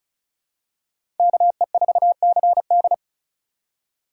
Morse code, heard as a keyed tone, about 36 wpm, 700 Hz. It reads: KE4CD
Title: USA Call Signs - 5 Characters - 2x Character Spacing - 35wpm